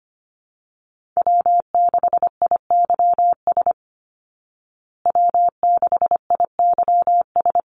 Morse code, heard as a keyed tone, about 25 wpm, 700 Hz. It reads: W6IYS W6IYS
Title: Call Signs Repeated 3x - 25wpm